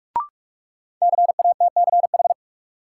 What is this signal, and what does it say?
Morse code, heard as a keyed tone, about 43 wpm, 700 Hz. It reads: CATCH
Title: Sentences from Top 1500 Words - New Word Drill - 45wpm